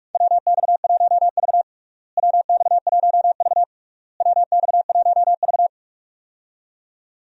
Morse code, heard as a keyed tone, about 42 wpm, 700 Hz. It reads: WX1V WX1V WX1V
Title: USA Call Signs - 4 Characters - Repeated 3x - 45wpm